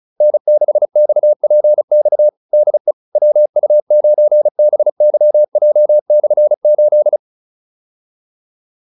Morse code, heard as a keyed tone, about 34 wpm, 600 Hz. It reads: N6XPX DE WU9BYJ/8